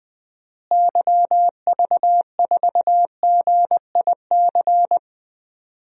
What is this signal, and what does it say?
Morse code, heard as a keyed tone, about 20 wpm, 700 Hz. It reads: YV4GIC
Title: DXCC-100 Call Signs - 6 Characters - Repeated 3x - 20wpm